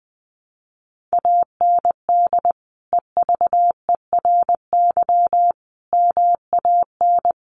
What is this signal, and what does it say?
Morse code, heard as a keyed tone, about 20 wpm, 700 Hz. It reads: AND EVERY MAN
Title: Sentences from Top 1500 Words 20wpm